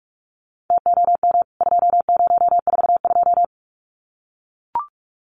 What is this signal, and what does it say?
Morse code, heard as a keyed tone, about 44 wpm, 700 Hz. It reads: TOM 2042
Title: CWT Exchange 45wpm